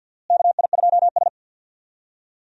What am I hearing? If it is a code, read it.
Morse code, heard as a keyed tone, about 52 wpm, 700 Hz. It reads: KI2S
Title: USA Call Signs - 4 Characters - Repeated 2x - 50wpm